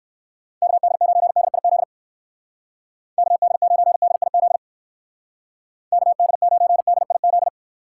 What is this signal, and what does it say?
Morse code, heard as a keyed tone, about 56 wpm, 700 Hz. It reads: KD9DIZ KD9DIZ KD9DIZ
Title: Speed Racing - USA 2x3 Call Signs 45wpm